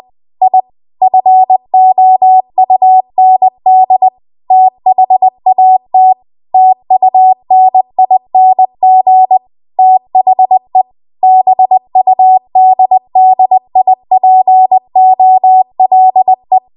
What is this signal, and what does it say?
Morse code, heard as a keyed tone, about 20 wpm, 750 Hz. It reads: I FOUND THAT TUNING THE BUDDIPOLE
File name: Archive / 20 WPM / 20WPM